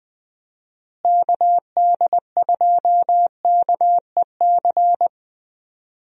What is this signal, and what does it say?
Morse code, heard as a keyed tone, about 20 wpm, 700 Hz. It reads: KD2KEC
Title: USA Call Signs - 6 Characters - Repeated 2x - 20wpm